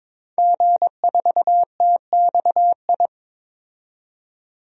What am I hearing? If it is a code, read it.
Morse code, heard as a keyed tone, about 22 wpm, 700 Hz. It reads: G4TXI
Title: DXCC-100 Call Signs - 5 Characters - 22wpm